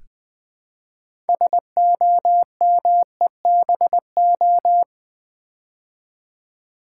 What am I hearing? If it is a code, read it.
Morse code, heard as a keyed tone, about 20 wpm, 700 Hz. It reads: SOMEBO
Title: Build A Word - 3 Letters At A Time - 8-Letter Words - 20wpm